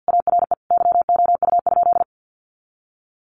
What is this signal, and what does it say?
Morse code, heard as a keyed tone, about 50 wpm, 700 Hz. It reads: ARE YOU?
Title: Sentences from Top 100 Words 50wpm